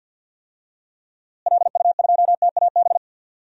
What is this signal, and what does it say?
Morse code, heard as a keyed tone, about 52 wpm, 700 Hz. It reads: LU2TAB
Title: DXCC-100 Call Signs - 6 Characters - Repeated 2x - 50wpm